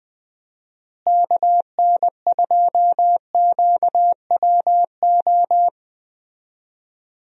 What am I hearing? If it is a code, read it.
Morse code, heard as a keyed tone, about 20 wpm, 700 Hz. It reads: KN2QWO
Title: USA Call Signs - 6 Characters - Repeated 2x - 20wpm